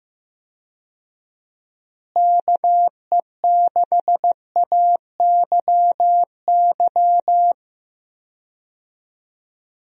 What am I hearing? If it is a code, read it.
Morse code, heard as a keyed tone, about 15 wpm, 700 Hz. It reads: KE6AYY
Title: USA Call Signs - 6 Characters - Repeated 2x - 15wpm